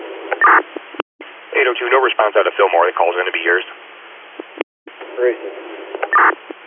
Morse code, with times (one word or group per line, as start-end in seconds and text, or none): none